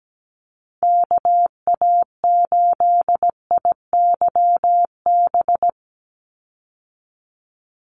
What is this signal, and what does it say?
Morse code, heard as a keyed tone, about 17 wpm, 700 Hz. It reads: KA8IYB
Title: USA Call Signs 17wpm